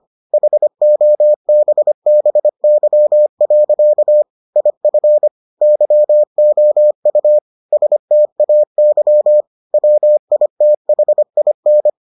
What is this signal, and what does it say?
Morse code, heard as a keyed tone, about 25 wpm, 600 Hz. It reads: HOBBY. IF YOU STAY WITHIN